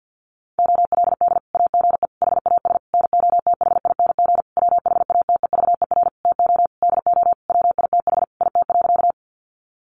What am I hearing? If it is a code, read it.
Morse code, heard as a keyed tone, about 50 wpm, 700 Hz. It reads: OLD AGE HAS NOTHING WHATEVER TO DO WITH IT.
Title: Sentences from Top 1100 Words 50wpm